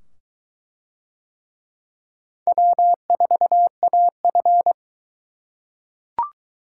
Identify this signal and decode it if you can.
Morse code, heard as a keyed tone, about 22 wpm, 700 Hz. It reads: W4AF